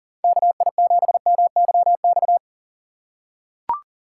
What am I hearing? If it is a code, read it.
Morse code, heard as a keyed tone, about 40 wpm, 700 Hz. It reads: KI7MYX
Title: USA Call Signs - 6 Characters - 8x Character Spacing - 40wpm